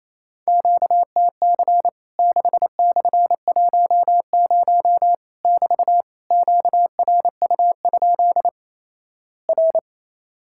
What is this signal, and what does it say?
Morse code, heard as a keyed tone, about 28 wpm, 700 Hz. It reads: QTC 6/10 <BT> QRU? R
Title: Worked All Europe DX Contest - QTC Exchange - 28wpm